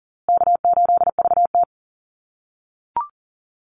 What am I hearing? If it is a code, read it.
Morse code, heard as a keyed tone, about 40 wpm, 700 Hz. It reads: K8VT
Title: USA Call Signs Repeated 2x 40wpm